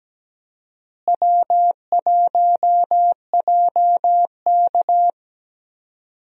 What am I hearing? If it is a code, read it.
Morse code, heard as a keyed tone, about 17 wpm, 700 Hz. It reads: W1JK
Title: USA Call Signs - 4 Characters - Repeated 3x - 17wpm